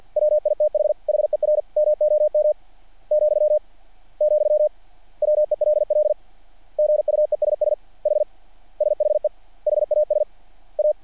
Morse code, heard as a keyed tone, about 50 wpm, 600 Hz. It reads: WITH HER MOM , , WELL GUESS H SHE HAS A
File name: archived / CRSnet